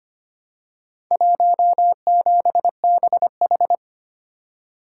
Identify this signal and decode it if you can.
Morse code, heard as a keyed tone, about 25 wpm, 700 Hz. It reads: 17BH
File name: practice / 25wpm / Four Letter-Number Combinations - 25wpm